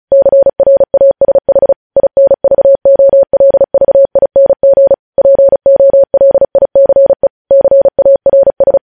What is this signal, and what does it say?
Morse code, heard as a keyed tone, about 36 wpm, 550 Hz. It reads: CRASH INVOLVING POLICE CARS
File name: podcast / News headlines-1x-35x35